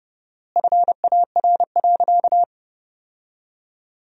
Morse code, heard as a keyed tone, about 30 wpm, 700 Hz. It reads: FAR.
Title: 4x Word-Spacing - Sentences from Top 400 Words - 30wpm